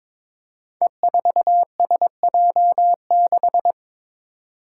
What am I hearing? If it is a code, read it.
Morse code, heard as a keyed tone, about 22 wpm, 700 Hz. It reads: E4SJ6
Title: Five Challenging Letter-Number Combinations 22wpm